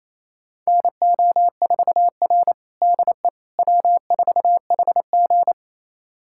Morse code, heard as a keyed tone, about 28 wpm, 700 Hz. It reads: NO4R DE W4HG
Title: USA Callsign DE Callsign - 4 Characters - 28wpm